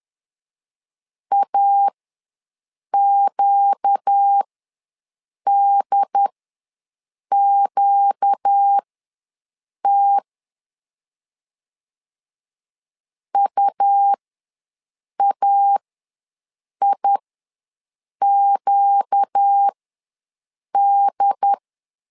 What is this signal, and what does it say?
Morse code, heard as a keyed tone, about 10 wpm, spaced as for 6 wpm, 800 Hz. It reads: AQDQT UAIQD
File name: LEZIONI / 4 Q ws 10wpm 2306